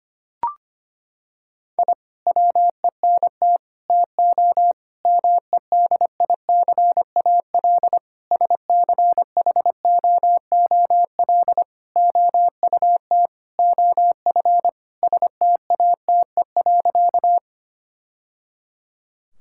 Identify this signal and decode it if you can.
Morse code, heard as a keyed tone, about 25 wpm, 700 Hz. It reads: I WENT TO MEDICAL SCHOOL OUT OF STATE.
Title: Sentences from Top 1000 Words - Review Part 5 - 25wpm